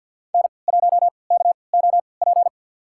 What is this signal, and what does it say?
Morse code, heard as a keyed tone, about 52 wpm, spaced as for 29 wpm, 700 Hz. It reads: N1KOP
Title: USA Call Signs - 5 Characters - 3x Character Spacing - 50wpm